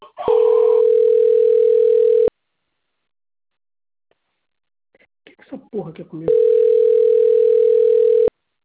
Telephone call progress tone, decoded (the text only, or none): ringback tone